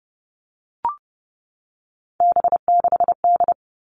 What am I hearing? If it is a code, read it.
Morse code, heard as a keyed tone, about 28 wpm, 700 Hz. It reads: B6D
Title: All Cut Numbers - 3-Digit 30wpm